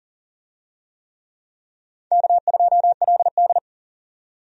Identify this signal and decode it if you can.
Morse code, heard as a keyed tone, about 40 wpm, 700 Hz. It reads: K2LD
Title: USA Call Signs - 4 Characters - 40wpm